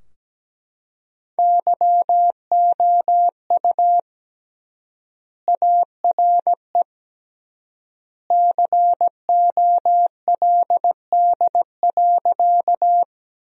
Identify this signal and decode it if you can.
Morse code, heard as a keyed tone, about 17 wpm, 700 Hz. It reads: YOU ARE COLD.
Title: 3x Word-Spacing - Sentences from Top 400 Words - 17wpm